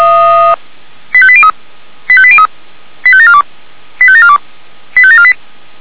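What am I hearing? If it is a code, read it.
Five-tone selective call (ZVEI), heard as quick runs of five tones, tones of about 70 ms. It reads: 85803 85803 85742 85742 85748
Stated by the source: ZVEI five-tone selective call received on an SDR.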